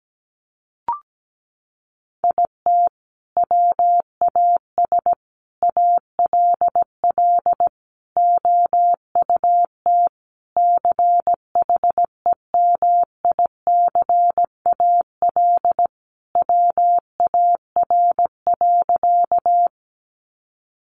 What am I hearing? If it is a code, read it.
Morse code, heard as a keyed tone, about 17 wpm, 700 Hz. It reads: IT WAS ALL OUT CHEMICAL WAR.
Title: Sentences from Top 1200 Words 17wpm